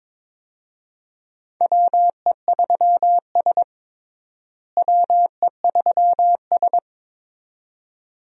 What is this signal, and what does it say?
Morse code, heard as a keyed tone, about 22 wpm, 700 Hz. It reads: WE3S WE3S
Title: USA Call Signs - 4 Characters - Repeated 2x - 22wpm